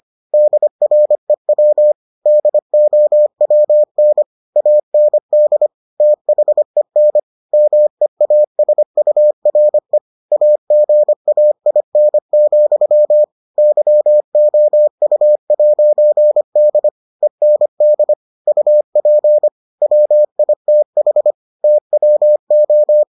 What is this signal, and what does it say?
Morse code, heard as a keyed tone, about 25 wpm, 600 Hz. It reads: DREW DOWN AND THEN MEASURE AGAIN, YOU'D END UP WITH TWO